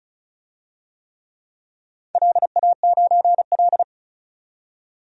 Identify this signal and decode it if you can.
Morse code, heard as a keyed tone, about 36 wpm, 690 Hz. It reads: LA9L
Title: DXCC-100 Call Signs - 4 Characters - 35wpm